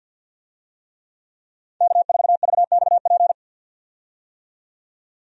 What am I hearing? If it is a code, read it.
Morse code, heard as a keyed tone, about 47 wpm, 700 Hz. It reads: K4VXP
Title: USA Call Signs - 5 Characters - 50wpm